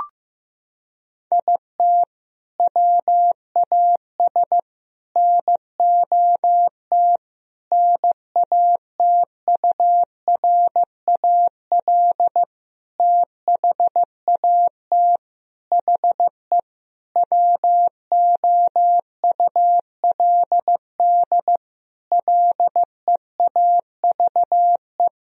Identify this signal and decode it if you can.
Morse code, heard as a keyed tone, about 15 wpm, 700 Hz. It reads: IT WAS NOT NATURAL THAT HE WOULD LEAVE
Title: Sentences from Top 1000 Words - Review Part 4 - 15wpm